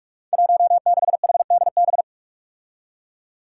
Morse code, heard as a keyed tone, about 44 wpm, 700 Hz. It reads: 16HDB
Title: Five Challenging Letter-Number Combinations 45wpm